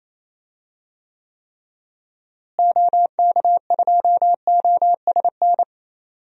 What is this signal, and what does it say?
Morse code, heard as a keyed tone, about 28 wpm, 700 Hz. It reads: OK2OSN
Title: DXCC-100 Call Signs - 6 Characters - Repeated 3x - 28wpm